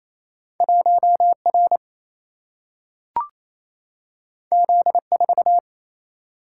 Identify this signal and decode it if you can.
Morse code, heard as a keyed tone, about 28 wpm, 700 Hz. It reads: 1 R Z 4
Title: Single-Letter Single-Number 28wpm